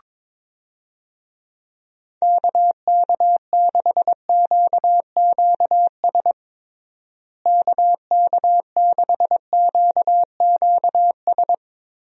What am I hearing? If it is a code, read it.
Morse code, heard as a keyed tone, about 22 wpm, 700 Hz. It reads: KK6QQS KK6QQS
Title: USA Call Signs - 6 Characters - Repeated 2x - 22wpm